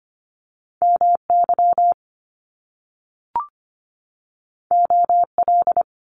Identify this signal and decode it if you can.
Morse code, heard as a keyed tone, about 25 wpm, 700 Hz. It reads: MY OL
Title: Top Two Letter Combinations in Words 25wpm